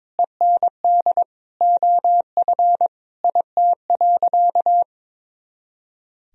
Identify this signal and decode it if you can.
Morse code, heard as a keyed tone, about 22 wpm, 700 Hz. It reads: END OF IT.
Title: Sentences from Top 200 Words and Abbreviations 22wpm